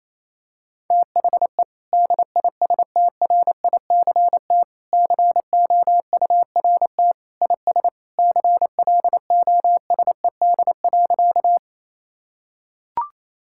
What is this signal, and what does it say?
Morse code, heard as a keyed tone, about 28 wpm, 700 Hz. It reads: THE DISTRICT COURT IS CLOSED.
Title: Sentences from Top 1600 Words 28wpm